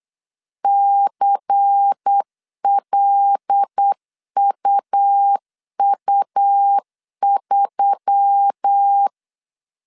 Morse code, heard as a keyed tone, about 8 wpm, 800 Hz. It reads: CLUU3
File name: 8wpm